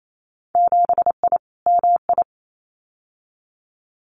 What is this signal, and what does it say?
Morse code, heard as a keyed tone, about 28 wpm, 700 Hz. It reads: 7I MI
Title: Winter Field Day Exchanges Encore 28wpm